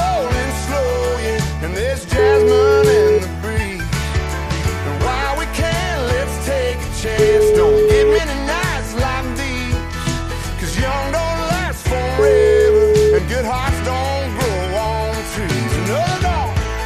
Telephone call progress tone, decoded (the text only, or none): ringback tone